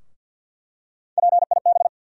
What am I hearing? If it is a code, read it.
Morse code, heard as a keyed tone, about 53 wpm, 700 Hz. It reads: PIB
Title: USA Call Sign Postfixes - 3 Letters 50wpm